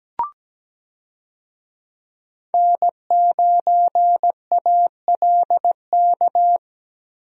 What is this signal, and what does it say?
Morse code, heard as a keyed tone, about 17 wpm, 700 Hz. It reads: N9ALK
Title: USA Call Signs - 5 Characters - Repeated 3x - 17wpm